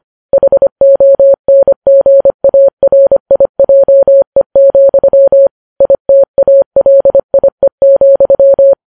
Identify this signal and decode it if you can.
Morse code, heard as a keyed tone, about 25 wpm, 550 Hz. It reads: HONGARIJE, ITALIE,